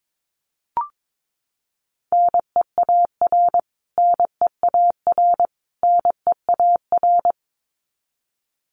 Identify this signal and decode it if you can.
Morse code, heard as a keyed tone, about 22 wpm, 700 Hz. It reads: NEAR NEAR NEAR
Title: Top 500 Words Repeated 22wpm